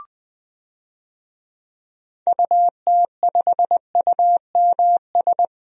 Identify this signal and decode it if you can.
Morse code, heard as a keyed tone, about 20 wpm, 700 Hz. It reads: UT5UMS
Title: DXCC-100 Call Signs - 6 Characters - 20wpm